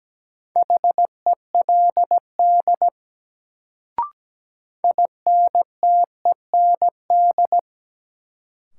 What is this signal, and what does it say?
Morse code, heard as a keyed tone, about 17 wpm, 700 Hz. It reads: HELD INTEND
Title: Sentences from Top 1600 Words - New Word Drill - 17wpm